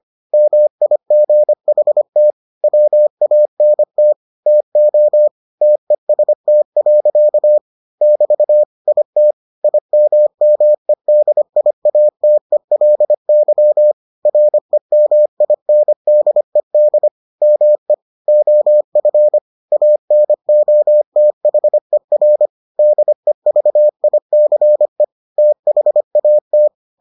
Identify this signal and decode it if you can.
Morse code, heard as a keyed tone, about 25 wpm, 600 Hz. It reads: MIGHT WANT TO TEST. <BT> IT IMMEDIATELY REMINDED ME OF ANOTHER DEVICE THAT